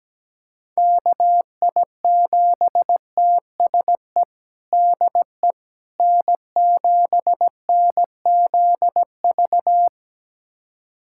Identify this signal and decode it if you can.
Morse code, heard as a keyed tone, about 17 wpm, 700 Hz. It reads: KI7TSE DE N7NZV
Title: USA Callsign DE Callsign 17wpm